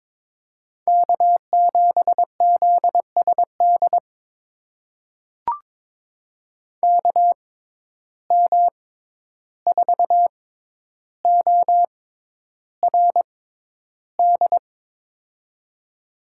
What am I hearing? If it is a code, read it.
Morse code, heard as a keyed tone, about 21 wpm, spaced as for 11 wpm, 700 Hz. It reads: K7ZSD KM4ORD
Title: USA Call Signs - 6x Character Spacing - 22wpm